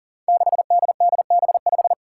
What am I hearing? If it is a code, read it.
Morse code, heard as a keyed tone, about 39 wpm, 700 Hz. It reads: 6DDB5
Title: Five Challenging Letter-Number Combinations 40wpm